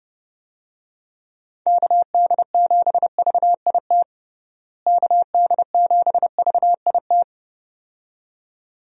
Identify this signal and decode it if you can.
Morse code, heard as a keyed tone, about 30 wpm, 700 Hz. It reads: KD7VIT KD7VIT
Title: USA Call Signs - 6 Characters - Repeated 2x - 30wpm